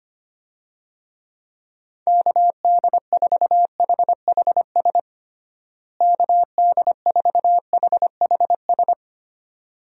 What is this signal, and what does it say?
Morse code, heard as a keyed tone, about 25 wpm, 700 Hz. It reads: KD4HHS KD4HHS
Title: USA Call Signs - 6 Characters - Repeated 2x - 25wpm